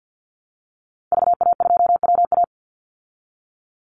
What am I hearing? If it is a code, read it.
Morse code, heard as a keyed tone, about 48 wpm, 700 Hz. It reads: VA2WA
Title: Call Signs Repeated 3x - 50wpm